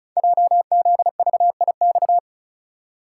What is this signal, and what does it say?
Morse code, heard as a keyed tone, about 36 wpm, 690 Hz. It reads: JZVIX
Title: Code Groups - 5 Letters - 35wpm